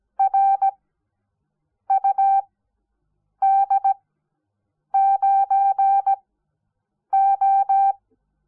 Morse code, heard as a keyed tone, about 17 wpm, spaced as for 7 wpm, 780 Hz. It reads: RUD9O
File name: Onlinelehrgang / morsen / DJ4UF Morsen05